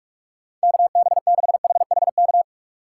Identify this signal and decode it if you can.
Morse code, heard as a keyed tone, about 46 wpm, 700 Hz. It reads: KB6HHK
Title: USA Call Signs - 6 Characters - 3x Character Spacing - 45wpm